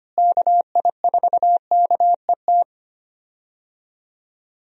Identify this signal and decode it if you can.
Morse code, heard as a keyed tone, about 25 wpm, 700 Hz. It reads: KI4KET
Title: USA Call Signs - 6 Characters - Repeated 3x - 25wpm